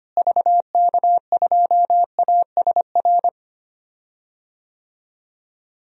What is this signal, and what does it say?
Morse code, heard as a keyed tone, about 25 wpm, 700 Hz. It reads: VK2ASR
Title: DXCC-100 Call Signs - 6 Characters - Repeated 2x - 25wpm